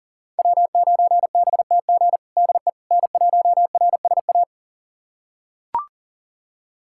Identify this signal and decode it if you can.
Morse code, heard as a keyed tone, about 40 wpm, 690 Hz. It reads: W9BTG DE N1RSA
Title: USA Callsign DE Callsign - 5 Characters - 40wpm